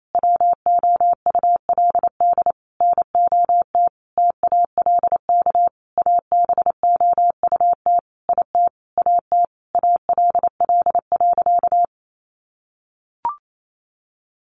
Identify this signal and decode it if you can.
Morse code, heard as a keyed tone, about 28 wpm, 700 Hz. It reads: WOULD NOT TALK ABOUT IT AT ALL.
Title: Sentences from Top 800 Words 28wpm